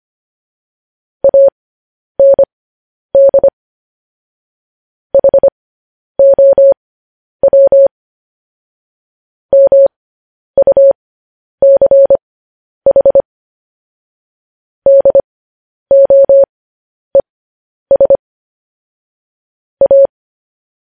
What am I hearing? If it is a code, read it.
Morse code, heard as a keyed tone, about 25 wpm, spaced as for 10 wpm, 550 Hz. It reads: AND HOW MUCH DOES A